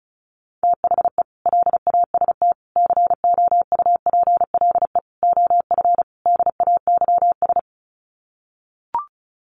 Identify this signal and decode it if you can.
Morse code, heard as a keyed tone, about 34 wpm, 700 Hz. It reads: THE LAST COUPLE OF DAYS
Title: Sets of 5 Words Encore 35wpm